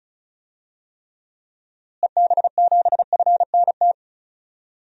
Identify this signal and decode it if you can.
Morse code, heard as a keyed tone, about 34 wpm, 700 Hz. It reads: EB7FNT
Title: DXCC-100 Call Signs - 6 Characters - Repeated 2x - 35wpm